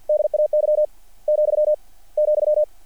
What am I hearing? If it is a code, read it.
Morse code, heard as a keyed tone, about 48 wpm, spaced as for 25 wpm, 660 Hz. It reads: DAY,,
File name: archived / CRSnet